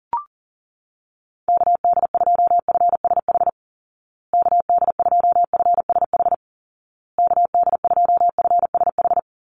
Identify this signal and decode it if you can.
Morse code, heard as a keyed tone, about 40 wpm, 700 Hz. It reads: KD2FSH KD2FSH KD2FSH
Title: Call Signs Repeated 2x - 40wpm